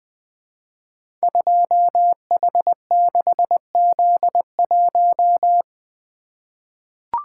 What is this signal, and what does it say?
Morse code, heard as a keyed tone, about 20 wpm, 700 Hz. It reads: 2H6Z1
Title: Code Groups - 5 Letter:Numbers - 20wpm